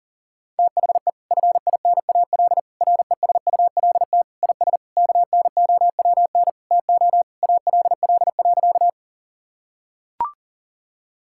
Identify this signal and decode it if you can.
Morse code, heard as a keyed tone, about 37 wpm, 700 Hz. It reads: THE FINAL RESULT IS KNOWN TO ALL.
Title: Sentences from Top 1000 Words - Review Part 5 - 40wpm